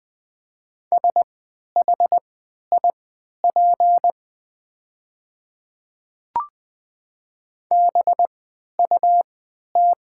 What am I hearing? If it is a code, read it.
Morse code, heard as a keyed tone, about 20 wpm, spaced as for 11 wpm, 700 Hz. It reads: SHIP BUT
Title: 3x Character Spacing - Top 500 Words - 20wpm